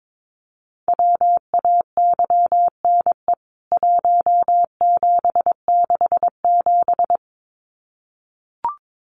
Morse code, heard as a keyed tone, about 22 wpm, 700 Hz. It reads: WAYNE 1767
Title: CWT Exchange 22wpm